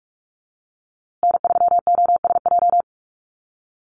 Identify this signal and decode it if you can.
Morse code, heard as a keyed tone, about 44 wpm, 700 Hz. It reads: N3OSJ
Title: USA Call Signs Repeated 2x 45wpm